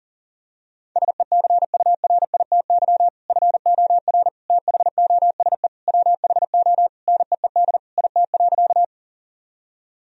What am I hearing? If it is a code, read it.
Morse code, heard as a keyed tone, about 40 wpm, 700 Hz. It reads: SECURITY FOR THOSE WHO NEED IT.
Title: Sentences from Top 1000 Words - Review Part 4 - 40wpm